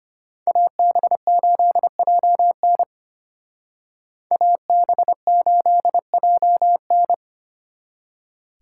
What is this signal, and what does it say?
Morse code, heard as a keyed tone, about 31 wpm, 700 Hz. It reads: AB8JN AB8JN
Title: Speed Racing - USA 2x2 Call Signs 25wpm